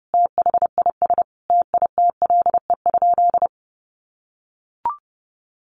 Morse code, heard as a keyed tone, about 30 wpm, 700 Hz. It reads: THIS TITLE?
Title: Sentences from Top 1200 Words 30wpm